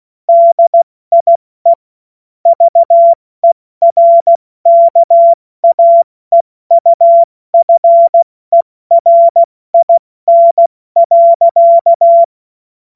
Morse code, heard as a keyed tone, about 16 wpm, 680 Hz. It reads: DIE VERKAEUFERIN.
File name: Morsekurs / Witze / Witz 15. November 2022 70 F80